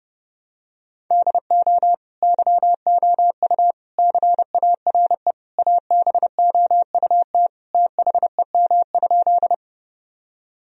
Morse code, heard as a keyed tone, about 32 wpm, 700 Hz. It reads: DO YOU CARE ABOUT THEM?
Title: Sentences from Top 1000 Words - Review Part 1 - 30wpm